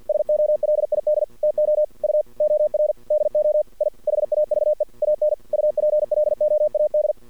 Morse code, heard as a keyed tone, about 50 wpm, 600 Hz. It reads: SOLID TO U OR DO I HAVE AN UPLOAD
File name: archived / CRSnet